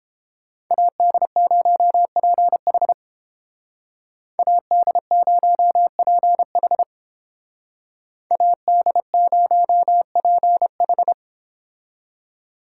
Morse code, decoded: AD0PH AD0PH AD0PH

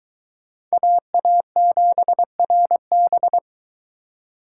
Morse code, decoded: AA7RB

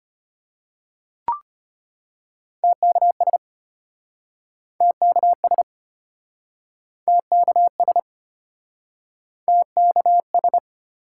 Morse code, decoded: TKS TKS TKS TKS

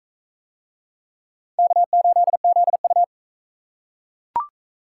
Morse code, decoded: K8ZU

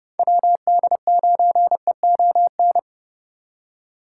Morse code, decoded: WD9EON

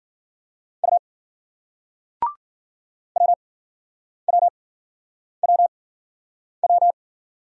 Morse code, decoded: U WWWW

